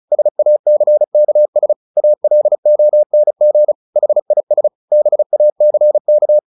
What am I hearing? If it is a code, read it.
Morse code, heard as a keyed tone, about 35 wpm, 600 Hz. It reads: SACKS ALONG HIS BACK
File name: the old man and the sea 2 35wpm